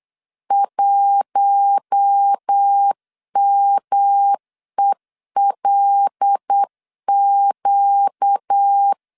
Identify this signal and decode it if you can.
Morse code, heard as a keyed tone, about 8 wpm, 800 Hz. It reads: 1MELQ
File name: LEZIONI / 6 PKX0 / 6 P 8wpm 3025